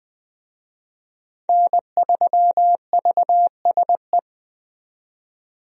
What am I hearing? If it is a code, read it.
Morse code, heard as a keyed tone, about 20 wpm, 700 Hz. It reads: N3VSE
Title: USA Call Signs - 5 Characters - 20wpm